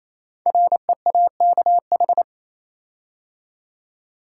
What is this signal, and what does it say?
Morse code, heard as a keyed tone, about 27 wpm, 700 Hz. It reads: REAKH
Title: Code Groups - 5 Letter:Numbers - 28wpm